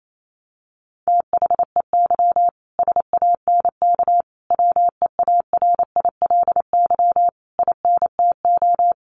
Morse code, heard as a keyed tone, about 28 wpm, 700 Hz. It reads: THEY SANK WEARILY INTO
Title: A Sound Of Thunder 28wpm